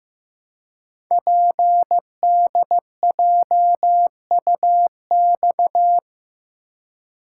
Morse code, heard as a keyed tone, about 15 wpm, 700 Hz. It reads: PDJUX
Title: Code Groups - 5 Letters - 15wpm